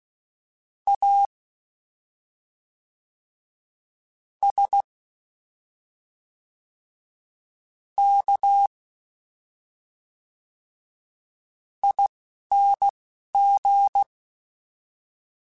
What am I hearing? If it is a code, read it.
Morse code, heard as a keyed tone, about 16 wpm, spaced as for 4 wpm, 800 Hz. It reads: ASKING
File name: Special INGs / Special INGs-mp3s / Fast ing @10